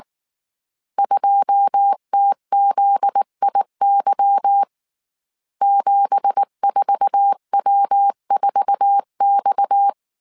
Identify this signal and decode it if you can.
Morse code, heard as a keyed tone, about 19 wpm, 800 Hz. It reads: 2TZIY 74W4X
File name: CW / DET / 1 9 1500C S A 18wpm 2459